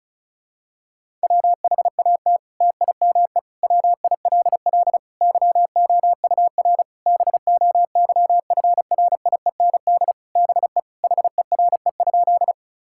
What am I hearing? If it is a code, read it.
Morse code, heard as a keyed tone, about 36 wpm, 700 Hz. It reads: WHAT TIME WILL YOUR BOYFRIEND BE HERE?
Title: Sentences from Top 1000 Words - Review Part 5 - 35wpm